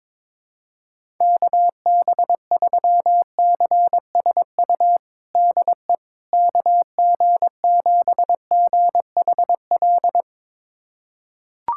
Morse code, heard as a keyed tone, about 22 wpm, 700 Hz. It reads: KB3CSU DE KG7GHL
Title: USA Callsign DE Callsign - 6 Characters - 22wpm